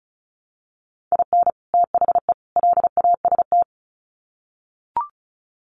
Morse code, heard as a keyed tone, about 34 wpm, 700 Hz. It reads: IN THE LAST
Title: Sets of 3 Words 35wpm